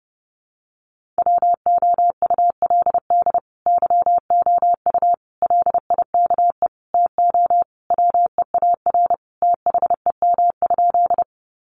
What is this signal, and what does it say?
Morse code, heard as a keyed tone, about 30 wpm, 700 Hz. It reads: WOULD YOU LIKE TO WEAR THEM?
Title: Sentences from Top 1500 Words 30wpm